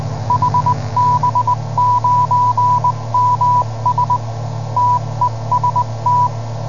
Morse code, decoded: HB9MS TEST